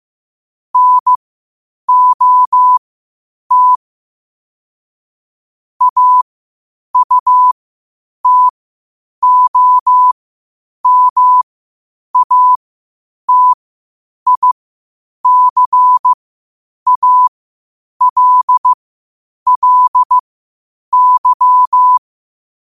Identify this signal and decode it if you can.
Morse code, heard as a keyed tone, about 15 wpm, spaced as for 8 wpm, 1000 Hz. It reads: NOT AUTOMATICALLY
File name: qotdfiles / 8wpm